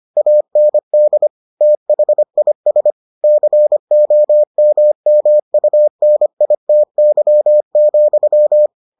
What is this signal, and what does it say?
Morse code, heard as a keyed tone, about 25 wpm, 600 Hz. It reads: AND THIS COMMUNITY,